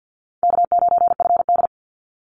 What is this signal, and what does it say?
Morse code, heard as a keyed tone, about 50 wpm, 700 Hz. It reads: K9FD
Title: Call Signs Repeated 2x - 50wpm